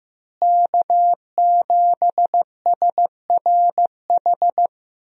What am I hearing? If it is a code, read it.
Morse code, heard as a keyed tone, about 15 wpm, 700 Hz. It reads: K7SRH
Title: USA Call Signs - 5 Characters - Repeated 3x - 15wpm